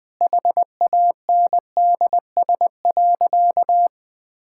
Morse code, heard as a keyed tone, about 20 wpm, 700 Hz. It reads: HANDS.
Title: Sentences with 7-Word Phrases - 20wpm